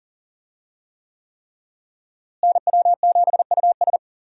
Morse code, heard as a keyed tone, about 37 wpm, 700 Hz. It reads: NW7US
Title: USA Call Signs - 5 Characters - Repeated 3x - 40wpm